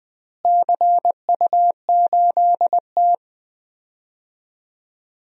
Morse code, heard as a keyed tone, about 20 wpm, 700 Hz. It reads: CU8T
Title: DXCC-100 Call Signs - 4 Characters - Repeated 3x - 20wpm